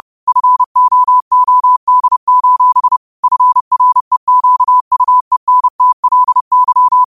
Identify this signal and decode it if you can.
Morse code, heard as a keyed tone, about 30 wpm, 1000 Hz. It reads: FOOD: FREQUENTLY
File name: qotdfiles / 30wpm